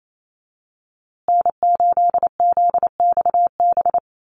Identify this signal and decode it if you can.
Morse code, heard as a keyed tone, about 28 wpm, 700 Hz. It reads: N8ZXB
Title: USA Call Signs Repeated 3x 28wpm